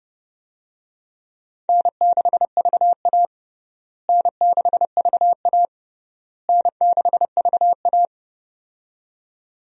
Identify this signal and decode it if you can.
Morse code, heard as a keyed tone, about 29 wpm, 700 Hz. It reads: N6VA N6VA N6VA
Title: USA Call Signs - 4 Characters - Repeated 3x - 30wpm